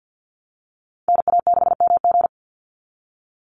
Morse code, heard as a keyed tone, about 50 wpm, 700 Hz. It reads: NA6MG
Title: NA SOTA Activator Callsigns 50wpm